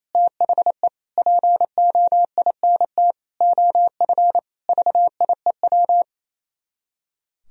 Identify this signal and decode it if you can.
Morse code, heard as a keyed tone, about 27 wpm, 700 Hz. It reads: THE POINT OF VIEW
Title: Speed-Racing - Sets of 5 Words 28wpm